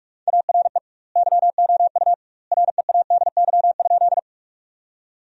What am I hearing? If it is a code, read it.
Morse code, heard as a keyed tone, about 46 wpm, 700 Hz. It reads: ARE YOU READY?
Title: 2x Word-Spacing - Sentences from Top 400 Words - 45wpm